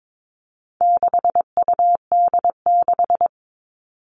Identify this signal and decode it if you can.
Morse code, heard as a keyed tone, about 22 wpm, 700 Hz. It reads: 6UD6
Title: All Cut Numbers - 4-Digit 22wpm